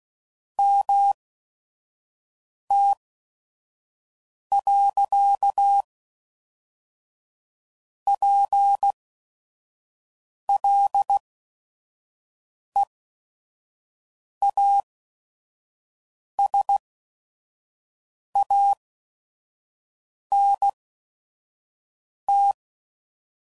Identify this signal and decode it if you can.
Morse code, heard as a keyed tone, about 16 wpm, spaced as for 5 wpm, 800 Hz. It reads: MT.PLEASANT
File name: CW Code Groups @5WPM / fromSM's / Cities